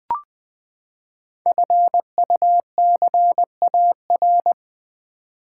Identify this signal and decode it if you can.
Morse code, heard as a keyed tone, about 20 wpm, 700 Hz. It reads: FUCAR
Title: Code Groups - 5 Letter:Numbers - 20wpm